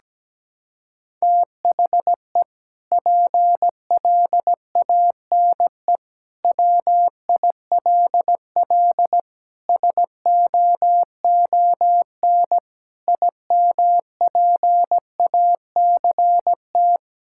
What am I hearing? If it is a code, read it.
Morse code, heard as a keyed tone, about 17 wpm, 700 Hz. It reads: THE PLANE WILL SOON IMPACT
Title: Sentences from Top 1000 Words - Review Part 6 - 17wpm